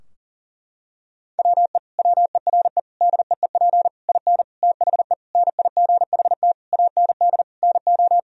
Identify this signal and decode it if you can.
Morse code, heard as a keyed tone, about 37 wpm, 700 Hz. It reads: WE WERE DEEP IN THE NIGHT AND NO
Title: Sentences from Top 1000 Words - Review Part 3 - 40wpm